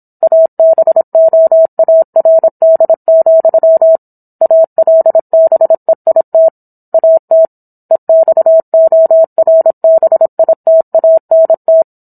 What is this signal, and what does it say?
Morse code, heard as a keyed tone, about 27 wpm, 650 Hz. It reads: ABOARD, ALBEIT AT EXORBITANT